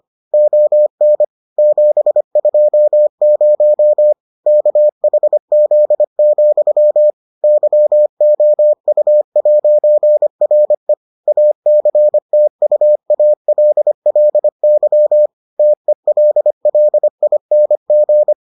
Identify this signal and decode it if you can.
Morse code, heard as a keyed tone, about 25 wpm, 600 Hz. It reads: ON 720 KHZ, YOU'RE ACTUALLY TELLING